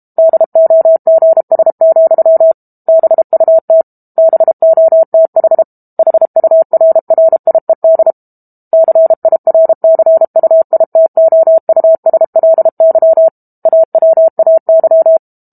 Morse code, heard as a keyed tone, about 32 wpm, 650 Hz. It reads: DOGS, BUT BOTH HURRIED CIRCUITOUSLY AWAY
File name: Book Two / 06-Chapter 6 - 32 WPM